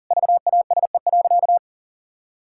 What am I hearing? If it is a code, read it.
Morse code, heard as a keyed tone, about 37 wpm, 700 Hz. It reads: VASE.
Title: Pangram Challenge v2 - 40wpm